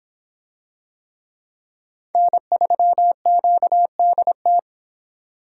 Morse code, heard as a keyed tone, about 24 wpm, 700 Hz. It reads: N3QDT